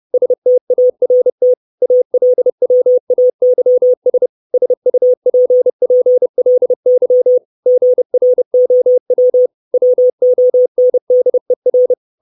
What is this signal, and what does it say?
Morse code, heard as a keyed tone, about 30 wpm, 500 Hz. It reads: START ALWAYS SUPPLY GROW WONDER
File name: 632 Words 30WPM